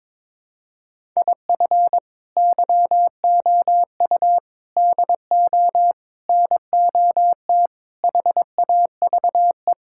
Morse code, heard as a keyed tone, about 21 wpm, 700 Hz. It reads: IF YOU DO NOT HAVE
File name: practice / 22wpm / Build A Phrase - 5-Word Phrases - 22wpm